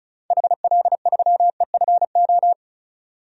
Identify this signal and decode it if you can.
Morse code, heard as a keyed tone, about 34 wpm, 700 Hz. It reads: HL3EFO
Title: DXCC-100 Call Signs - 6 Characters - 35wpm